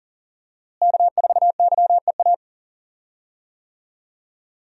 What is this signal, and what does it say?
Morse code, heard as a keyed tone, about 40 wpm, 700 Hz. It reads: K4YEA
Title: USA Call Signs - 5 Characters - Repeated 3x - 40wpm